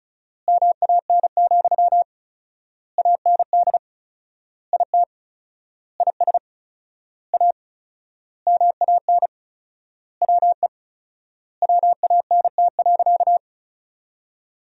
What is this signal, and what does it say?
Morse code, heard as a keyed tone, about 34 wpm, 700 Hz. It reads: MAN, AND IT IS A MAN WE WANT.
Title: 4x Word-Spacing - Sentences from Top 200 Words - 35wpm